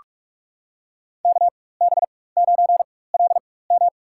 Morse code, heard as a keyed tone, about 44 wpm, spaced as for 21 wpm, 700 Hz. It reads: KB9LM
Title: USA Call Signs - 5 Characters - 4x Character Spacing - 45wpm